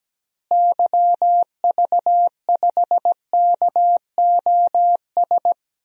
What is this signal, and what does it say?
Morse code, heard as a keyed tone, about 17 wpm, 700 Hz. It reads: YV5KOS
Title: DXCC-100 Call Signs - 6 Characters - Repeated 2x - 17wpm